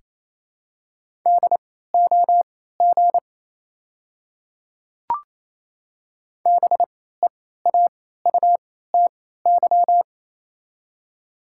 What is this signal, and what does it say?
Morse code, heard as a keyed tone, about 28 wpm, spaced as for 16 wpm, 700 Hz. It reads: DOG BEAUTY
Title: 3x Character Spacing - Top 500 Words - 28wpm